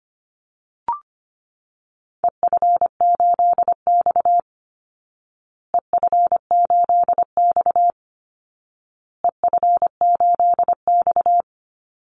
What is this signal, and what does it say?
Morse code, heard as a keyed tone, about 25 wpm, 700 Hz. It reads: EF8X EF8X EF8X
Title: Call Signs Repeated 2x - 25wpm